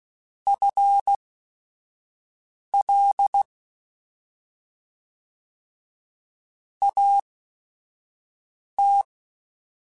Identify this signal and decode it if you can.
Morse code, heard as a keyed tone, about 16 wpm, spaced as for 5 wpm, 800 Hz. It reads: FL AT
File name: FL / FL mp3s / Fast FL @5